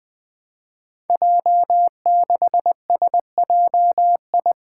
Morse code, heard as a keyed tone, about 20 wpm, 700 Hz. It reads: J6SJI
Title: Five Challenging Letter-Number Combinations 20wpm